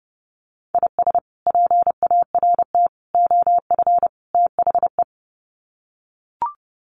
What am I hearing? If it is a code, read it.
Morse code, heard as a keyed tone, about 30 wpm, 700 Hz. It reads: IS PART OF THE
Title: Sets of 4 Words 30wpm